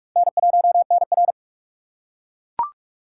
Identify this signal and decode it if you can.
Morse code, heard as a keyed tone, about 40 wpm, 700 Hz. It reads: N1NR